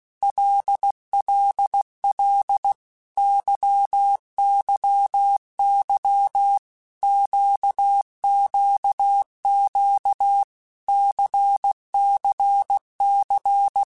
Morse code, heard as a keyed tone, about 16 wpm, 800 Hz. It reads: LLL YYY QQQ CCC
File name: Basic Letter Groups By7Groups / LFCYQ / LFCYQ@15